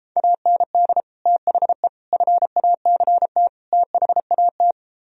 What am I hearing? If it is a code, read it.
Morse code, heard as a keyed tone, about 29 wpm, 700 Hz. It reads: AND THE FACT THAT